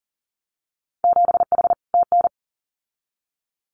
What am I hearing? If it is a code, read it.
Morse code, heard as a keyed tone, about 42 wpm, 700 Hz. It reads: 7H TN
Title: Winter Field Day Exchanges Encore 40wpm